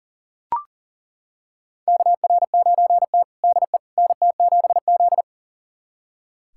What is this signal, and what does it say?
Morse code, heard as a keyed tone, about 40 wpm, 700 Hz. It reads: KR9T DE NT7Z
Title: USA Callsign DE Callsign - 4 Characters - 40wpm